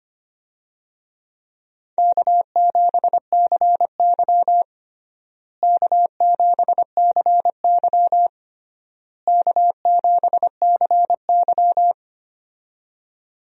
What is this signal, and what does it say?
Morse code, heard as a keyed tone, about 25 wpm, 700 Hz. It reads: K7CY K7CY K7CY
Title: USA Call Signs - 4 Characters - Repeated 3x - 25wpm